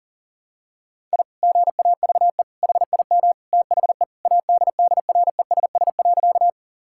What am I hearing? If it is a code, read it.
Morse code, heard as a keyed tone, about 37 wpm, 700 Hz. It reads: I GAVE HIM THE ADDRESS.
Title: Sentences from Top 1000 Words - Review Part 6 - 40wpm